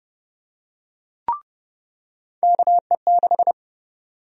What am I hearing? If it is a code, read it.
Morse code, heard as a keyed tone, about 32 wpm, 700 Hz. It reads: KE6
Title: USA Call Sign Prefixes - 30wpm